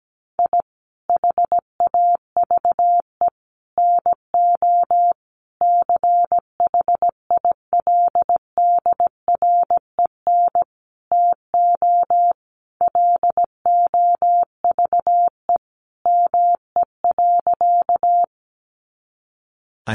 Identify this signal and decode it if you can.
Morse code, heard as a keyed tone, about 17 wpm, 700 Hz. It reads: I HAVE NO CHILDREN TO LOVE ME.
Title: Sentences from Top 400 Words 17wpm